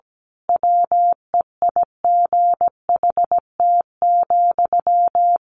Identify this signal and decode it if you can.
Morse code, heard as a keyed tone, about 17 wpm, 700 Hz. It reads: WEIGHT,
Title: A Sound Of Thunder 17wpm